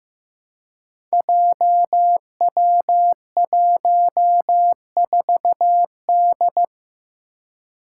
Morse code, heard as a keyed tone, about 15 wpm, 700 Hz. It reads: JW14D
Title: Five Challenging Letter-Number Combinations 15wpm